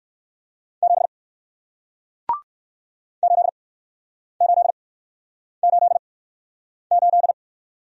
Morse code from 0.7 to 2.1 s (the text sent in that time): 7